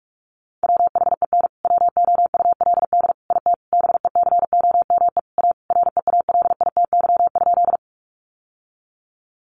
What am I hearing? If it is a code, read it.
Morse code, heard as a keyed tone, about 44 wpm, 700 Hz. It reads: WHEN WOULD IT BECOME A REALITY?
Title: Sentences from Top 1200 Words 45wpm